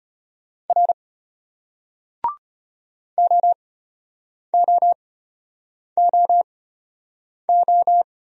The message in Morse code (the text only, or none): R OOOO